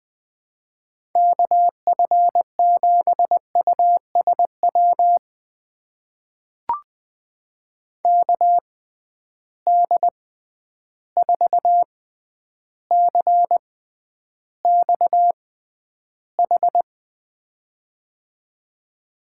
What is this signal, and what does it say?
Morse code, heard as a keyed tone, about 20 wpm, spaced as for 10 wpm, 700 Hz. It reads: KF7USW KD4CXH
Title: USA Call Signs - 6 Characters - 6x Character Spacing - 20wpm